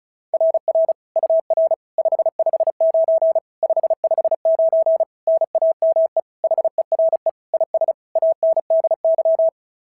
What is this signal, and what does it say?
Morse code, heard as a keyed tone, about 36 wpm, 660 Hz. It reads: RR UR 559 559 NAME HERE IS ANDY
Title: SOTA QSO Practice 35wpm